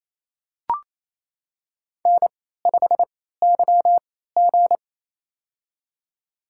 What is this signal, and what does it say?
Morse code, heard as a keyed tone, about 28 wpm, spaced as for 16 wpm, 700 Hz. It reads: N5YG
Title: USA Call Signs - 4 Characters - 3x Character Spacing - 28wpm